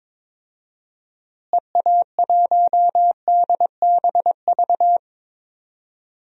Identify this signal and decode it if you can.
Morse code, heard as a keyed tone, about 22 wpm, 700 Hz. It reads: EA1DBV
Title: DXCC-100 Call Signs - 6 Characters - Repeated 3x - 22wpm